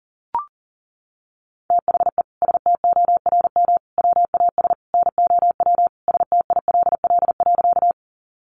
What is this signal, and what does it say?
Morse code, heard as a keyed tone, about 40 wpm, 700 Hz. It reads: THE STORM WAS NOW STILL.
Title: Sentences from Top 1500 Words 40wpm